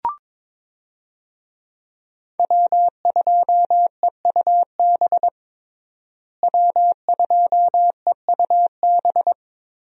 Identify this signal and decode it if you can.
Morse code, heard as a keyed tone, about 22 wpm, 700 Hz. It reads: W2EUB W2EUB
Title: USA Call Signs - 5 Characters - Repeated 2x - 22wpm